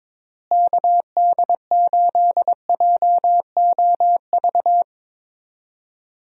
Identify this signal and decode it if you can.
Morse code, heard as a keyed tone, about 22 wpm, 700 Hz. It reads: KD8JOV
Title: USA Call Signs - 6 Characters - Repeated 3x - 22wpm